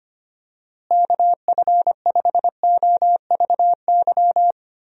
Morse code, heard as a keyed tone, about 25 wpm, 700 Hz. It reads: KF5OVY
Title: USA Call Signs - 6 Characters - Repeated 2x - 25wpm